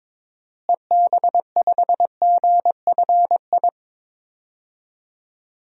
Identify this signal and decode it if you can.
Morse code, heard as a keyed tone, about 22 wpm, 700 Hz. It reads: EB5GFI
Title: DXCC-100 Call Signs - 6 Characters - Repeated 2x - 22wpm